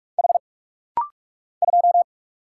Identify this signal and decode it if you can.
Morse code, heard as a keyed tone, about 44 wpm, 700 Hz. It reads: H 2
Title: Single Letter-Number - Rapid-Fire 45wpm